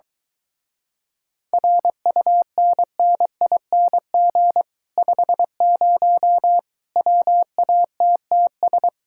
Morse code, heard as a keyed tone, about 22 wpm, 700 Hz. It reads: RUNNING 50 WATTS